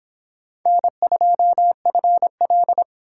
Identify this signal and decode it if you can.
Morse code, heard as a keyed tone, about 25 wpm, 700 Hz. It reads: N2FL